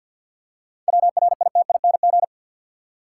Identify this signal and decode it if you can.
Morse code, heard as a keyed tone, about 53 wpm, 700 Hz. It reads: WRITING
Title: Seven Letter Words - 50wpm